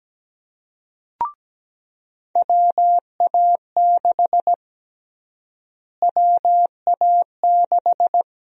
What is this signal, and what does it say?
Morse code, heard as a keyed tone, about 17 wpm, 700 Hz. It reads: WA6 WA6
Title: USA Call Sign Prefixes - Repeated 2x - 17wpm